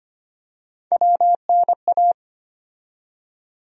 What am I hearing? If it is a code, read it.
Morse code, heard as a keyed tone, about 25 wpm, 700 Hz. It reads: WNA
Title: USA Call Sign Postfixes - 3 Letters 25wpm